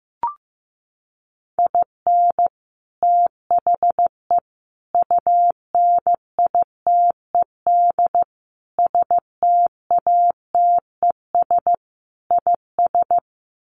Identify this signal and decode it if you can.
Morse code, heard as a keyed tone, about 15 wpm, 700 Hz. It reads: IN THE UNITED STATES IS
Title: Sets of 5 Words 15wpm